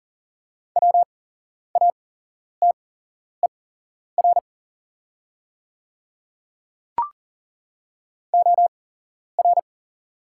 Morse code, heard as a keyed tone, about 40 wpm, spaced as for 11 wpm, 690 Hz. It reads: WATER OR
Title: 8x Character Spacing - Top 500 Words - 40wpm